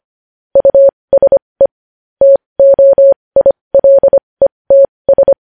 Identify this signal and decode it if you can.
Morse code, heard as a keyed tone, about 25 wpm, spaced as for 20 wpm, 550 Hz. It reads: USE TOILETS
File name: podcast / News headlines-1x-25x20